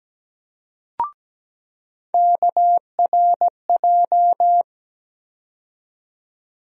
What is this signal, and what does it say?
Morse code, heard as a keyed tone, about 17 wpm, 700 Hz. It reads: KRJ
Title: USA Call Sign Postfixes - 3 Letters 17wpm